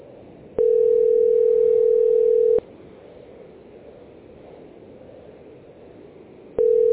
Ringback tone (North America), 440 + 480 Hz.